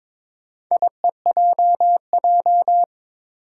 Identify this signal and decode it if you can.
Morse code, heard as a keyed tone, about 22 wpm, 700 Hz. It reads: IEJJ
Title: Four Challenging Letter-Number Combinations 22wpm